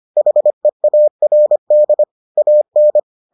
Morse code, heard as a keyed tone, about 25 wpm, 600 Hz. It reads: HEARD AN